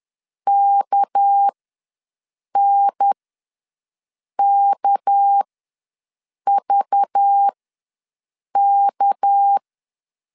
Morse code, heard as a keyed tone, about 10 wpm, spaced as for 6 wpm, 800 Hz. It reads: KNKVK
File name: LEZIONI / 6 PKX0 / 6 PK ws 10wpm 3745